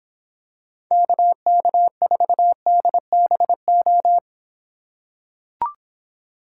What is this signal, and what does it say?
Morse code, heard as a keyed tone, about 26 wpm, 700 Hz. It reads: KK4DBO